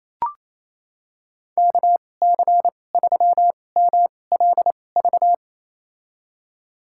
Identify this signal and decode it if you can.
Morse code, heard as a keyed tone, about 28 wpm, spaced as for 20 wpm, 700 Hz. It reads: KC3MLV
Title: USA Call Signs - 6 Characters - 2x Character Spacing - 28wpm